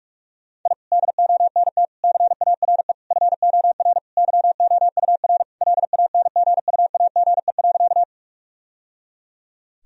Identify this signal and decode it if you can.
Morse code, heard as a keyed tone, about 42 wpm, 700 Hz. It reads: I DONT CARE FOR YOUR LANGUAGE.
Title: Sentences from Top 1000 Words - Review Part 3 - 45wpm